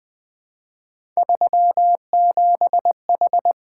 Morse code, heard as a keyed tone, about 20 wpm, 700 Hz. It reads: 37H
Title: Three Letter-Number Combinations - 20wpm